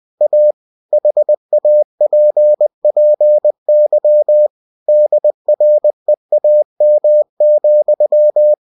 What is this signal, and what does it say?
Morse code, heard as a keyed tone, about 20 wpm, 600 Hz. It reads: A HAPPY DREAM,
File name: cw-books / winnie-the-pooh / winnie-20wpm-page-0037